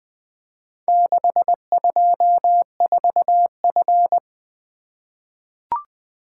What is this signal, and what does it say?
Morse code, heard as a keyed tone, about 20 wpm, 700 Hz. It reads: 624F
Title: Four Letter-Number Combinations - 20wpm